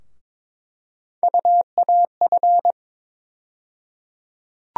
Morse code, heard as a keyed tone, about 21 wpm, 700 Hz. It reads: UAF
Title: USA Call Sign Postfixes - 3 Letters 22wpm